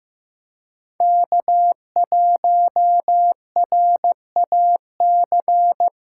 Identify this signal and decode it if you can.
Morse code, heard as a keyed tone, about 15 wpm, 700 Hz. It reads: K1RAC
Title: USA Call Signs - 5 Characters - 3x Character Spacing - 15wpm